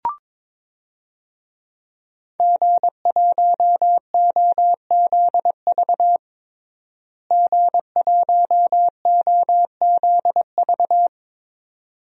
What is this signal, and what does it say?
Morse code, heard as a keyed tone, about 21 wpm, 700 Hz. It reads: G1OZV G1OZV
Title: DXCC-100 Call Signs - 5 Characters - Repeated 2x - 22wpm